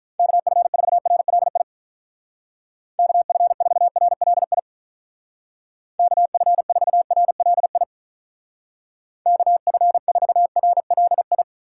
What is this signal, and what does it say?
Morse code, heard as a keyed tone, about 45 wpm, 700 Hz. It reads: KF4RLI KF4RLI KF4RLI KF4RLI